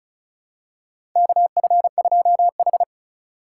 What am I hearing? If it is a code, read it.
Morse code, heard as a keyed tone, about 36 wpm, 700 Hz. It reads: KF2H
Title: USA Call Signs - 4 Characters - 2x Character Spacing - 35wpm